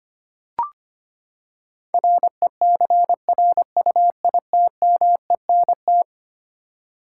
Morse code, heard as a keyed tone, about 25 wpm, 700 Hz. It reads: RECRUITMENT
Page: morse-code-ninja-practice-25wpm